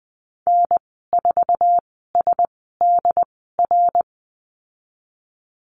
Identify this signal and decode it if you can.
Morse code, heard as a keyed tone, about 20 wpm, spaced as for 15 wpm, 700 Hz. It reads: N4SDR
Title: 069 Lesson 12g - Letter d - Call Signs 20wpm